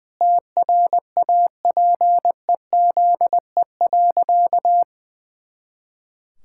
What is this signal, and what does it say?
Morse code, heard as a keyed tone, about 20 wpm, 700 Hz. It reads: TRAPEZE.
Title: Pangram Challenge v2 - 20wpm